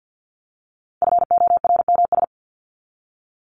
Morse code, heard as a keyed tone, about 50 wpm, 700 Hz. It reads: FORMS
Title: ES:S Suffix - Top 100 Words with Base Words 50wpm